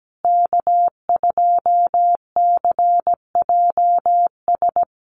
Morse code, heard as a keyed tone, about 17 wpm, 700 Hz. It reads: K2CJS
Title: USA Call Signs Repeated 3x 17wpm